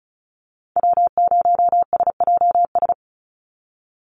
Morse code, heard as a keyed tone, about 34 wpm, 700 Hz. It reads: W0SJS
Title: Call Signs Repeated 3x - 35wpm